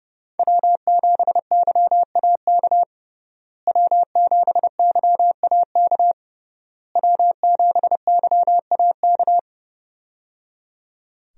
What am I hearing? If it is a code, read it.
Morse code, heard as a keyed tone, about 30 wpm, 700 Hz. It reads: W7YAK W7YAK W7YAK
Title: USA Call Signs - 5 Characters - Repeated 3x - 30wpm